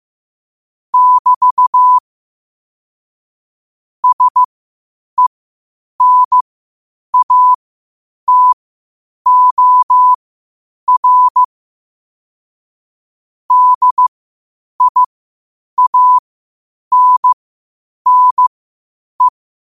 Morse code, heard as a keyed tone, about 15 wpm, spaced as for 8 wpm, 1000 Hz. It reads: <BT> SENATOR DIANNE
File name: qotdfiles / 8wpm